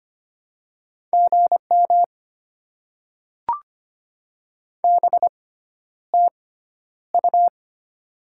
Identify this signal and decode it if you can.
Morse code, heard as a keyed tone, about 25 wpm, spaced as for 11 wpm, 700 Hz. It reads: GM BTU
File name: Top 20 QSO Elements - 6x Character Spacing - 25wpm